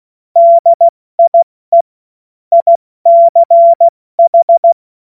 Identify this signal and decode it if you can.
Morse code, heard as a keyed tone, about 16 wpm, 680 Hz. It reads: DIE ICH
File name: Morsekurs / Witze / Witz 28. März 2023 70 F80